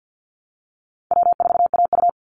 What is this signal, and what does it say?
Morse code, heard as a keyed tone, about 50 wpm, 700 Hz. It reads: W4AU
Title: Easy Call Signs Repeated 2x - 50wpm